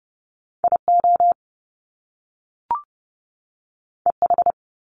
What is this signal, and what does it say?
Morse code, heard as a keyed tone, about 30 wpm, 700 Hz. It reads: IO EH
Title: Top Two Letter Combinations in Words 30wpm